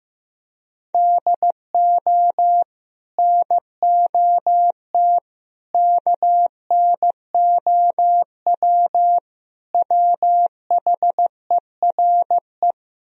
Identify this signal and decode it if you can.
Morse code, heard as a keyed tone, about 15 wpm, 700 Hz. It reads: DO NOT KNOW WHERE
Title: Speed-Racing - Sets of 4 Words 15wpm